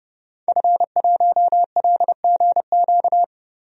Morse code, heard as a keyed tone, about 30 wpm, 700 Hz. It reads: F1LGQ
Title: DXCC-100 Call Signs - 5 Characters - Repeated 2x - 30wpm